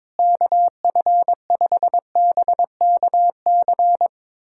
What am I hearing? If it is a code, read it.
Morse code, heard as a keyed tone, about 22 wpm, 700 Hz. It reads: KF5BKC
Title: USA Callsign DE Callsign - 6 Characters - 22wpm